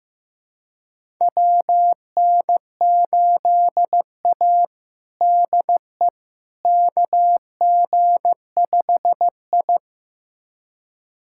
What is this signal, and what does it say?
Morse code, heard as a keyed tone, about 15 wpm, 700 Hz. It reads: WN8A DE KG5I
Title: USA Callsign DE Callsign - 4 Characters - 15wpm